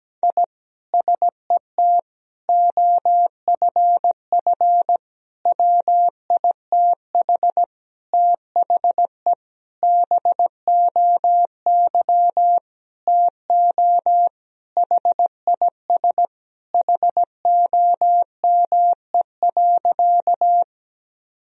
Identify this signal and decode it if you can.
Morse code, heard as a keyed tone, about 17 wpm, 700 Hz. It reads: I SET OFF WITH THE BOY TO HIS HOME.
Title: Sentences from Top 1000 Words - Review Part 1 - 17wpm